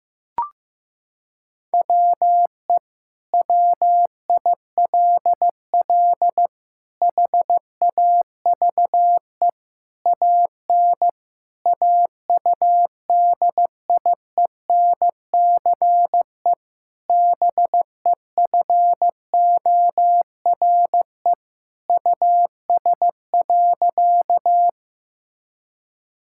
Morse code, heard as a keyed tone, about 15 wpm, 700 Hz. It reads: WE WILL HAVE AN AUDIENCE BEFORE US.
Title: Sentences from Top 1000 Words - Review Part 5 - 15wpm